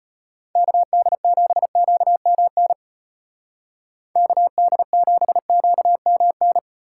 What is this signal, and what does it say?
Morse code, heard as a keyed tone, about 31 wpm, 700 Hz. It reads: KD7QMN KD7QMN
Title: Speed Racing - USA 2x3 Call Signs 25wpm